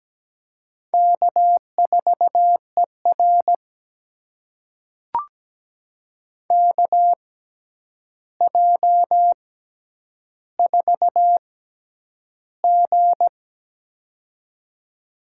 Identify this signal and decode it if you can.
Morse code, heard as a keyed tone, about 17 wpm, spaced as for 9 wpm, 700 Hz. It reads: K4ER KJ4G
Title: USA Call Signs - 4 Characters - 6x Character Spacing - 17wpm